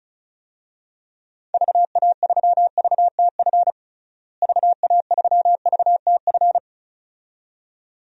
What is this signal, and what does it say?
Morse code, heard as a keyed tone, about 36 wpm, 700 Hz. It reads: VA3VTF VA3VTF
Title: DXCC-100 Call Signs - 6 Characters - Repeated 2x - 35wpm